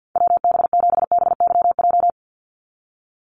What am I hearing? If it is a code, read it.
Morse code, heard as a keyed tone, about 50 wpm, 700 Hz. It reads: WB7BYJ
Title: USA Call Signs 50wpm